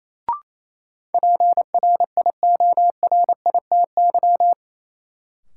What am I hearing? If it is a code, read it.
Morse code, heard as a keyed tone, about 28 wpm, 700 Hz. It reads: PRIORITY
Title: Sentences from Top 1600 Words - New Word Drill Alt - 28wpm